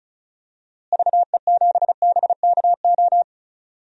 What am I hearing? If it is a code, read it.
Morse code, heard as a keyed tone, about 34 wpm, 700 Hz. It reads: VE7BKO
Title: DXCC-100 Call Signs - 6 Characters - Repeated 3x - 35wpm